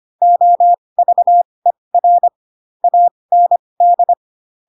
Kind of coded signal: Morse code